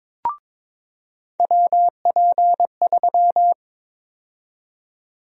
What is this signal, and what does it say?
Morse code, heard as a keyed tone, about 22 wpm, 700 Hz. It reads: WP3
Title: USA Call Sign Prefixes - 22wpm